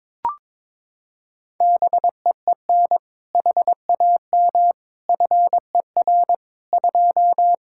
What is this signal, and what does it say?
Morse code, heard as a keyed tone, about 22 wpm, 700 Hz. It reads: BEEN HAM FER 2
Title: Practice QSO - Exchange 3 - More Info - 22wpm